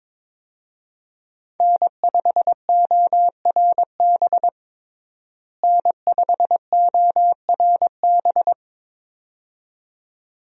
Morse code, heard as a keyed tone, about 22 wpm, 700 Hz. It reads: N5ORB N5ORB
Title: USA Call Signs - 5 Characters - Repeated 2x - 22wpm